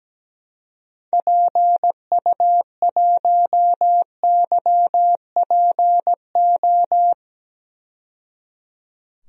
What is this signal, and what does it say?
Morse code, heard as a keyed tone, about 17 wpm, 700 Hz. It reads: PU1YPO
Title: DXCC-100 Call Signs - 6 Characters - 17wpm